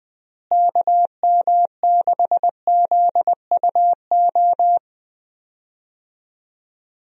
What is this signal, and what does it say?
Morse code, heard as a keyed tone, about 20 wpm, 700 Hz. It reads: KM6ZUO
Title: USA Call Signs - 6 Characters - Repeated 2x - 20wpm